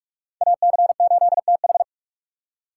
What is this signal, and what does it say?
Morse code, heard as a keyed tone, about 46 wpm, 700 Hz. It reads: AC8TH
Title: USA Call Signs - 5 Characters - 3x Character Spacing - 45wpm